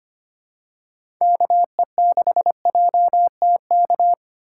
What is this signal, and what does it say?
Morse code, heard as a keyed tone, about 25 wpm, 700 Hz. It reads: KE6JTK
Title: USA Call Signs - 6 Characters - 3x Character Spacing - 25wpm